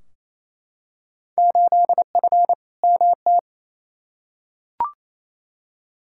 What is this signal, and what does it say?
Morse code, heard as a keyed tone, about 28 wpm, 700 Hz. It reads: 8F MT
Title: Field Day Exchanges Encore - 28wpm